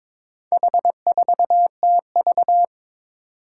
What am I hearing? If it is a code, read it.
Morse code, heard as a keyed tone, about 22 wpm, 700 Hz. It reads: H4TV
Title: Four Challenging Letter-Number Combinations 22wpm